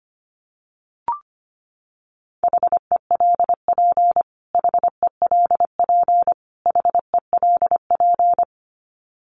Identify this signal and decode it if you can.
Morse code, heard as a keyed tone, about 25 wpm, 700 Hz. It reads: HELP HELP HELP
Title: Top 200 Words Repeated 25wpm